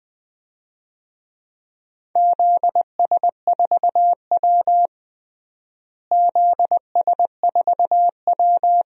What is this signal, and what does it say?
Morse code, heard as a keyed tone, about 20 wpm, 700 Hz. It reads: ZS4W ZS4W
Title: DXCC-100 Call Signs - 4 Characters - Repeated 3x - 20wpm